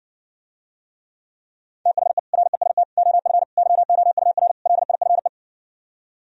Morse code, heard as a keyed tone, about 69 wpm, 700 Hz. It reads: THE REST OF YOUR LIFE